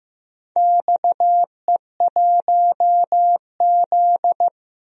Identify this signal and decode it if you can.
Morse code, heard as a keyed tone, about 15 wpm, 700 Hz. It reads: XE1Z
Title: DXCC-100 Call Signs - 4 Characters - 15wpm